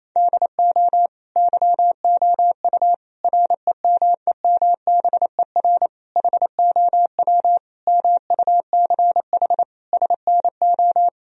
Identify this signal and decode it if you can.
Morse code, heard as a keyed tone, about 28 wpm, 700 Hz. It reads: DO YOU REMEMBER HOW MUCH SNO
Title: Sentences from Top 1000 Words - Review Part 3 - 28wpm